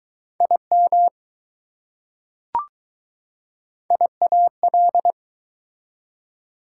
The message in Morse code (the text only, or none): IM IAL